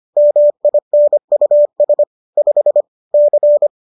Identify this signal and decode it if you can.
Morse code, heard as a keyed tone, about 25 wpm, 590 Hz. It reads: MINUS 5 C
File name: qso219 25